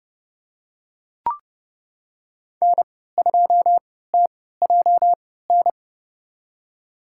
Morse code, heard as a keyed tone, about 29 wpm, spaced as for 17 wpm, 700 Hz. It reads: N2TJN
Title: USA Call Signs - 5 Characters - 3x Character Spacing - 30wpm